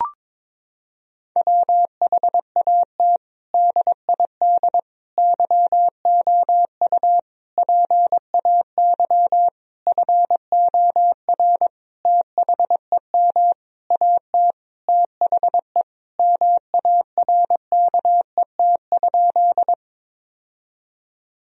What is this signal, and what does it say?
Morse code, heard as a keyed tone, about 22 wpm, 700 Hz. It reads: WHAT DID YOU PAY FOR THEM AT THE MARKET?
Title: Sentences from Top 1000 Words - Review Part 3 - 22wpm